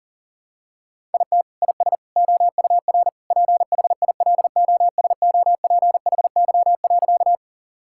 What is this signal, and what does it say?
Morse code, heard as a keyed tone, about 40 wpm, 700 Hz. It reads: IT IS OUR PHILOSOPHY.
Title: Sentences from Top 1000 Words - Review Part 6 - 40wpm